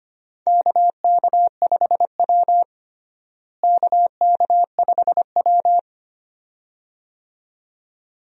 Morse code, heard as a keyed tone, about 25 wpm, 700 Hz. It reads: KK5W KK5W
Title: USA Call Signs - 4 Characters - Repeated 2x - 25wpm